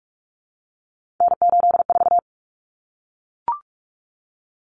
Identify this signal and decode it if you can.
Morse code, heard as a keyed tone, about 44 wpm, 700 Hz. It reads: N84
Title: Common Cut Numbers - 3-Digit 45wpm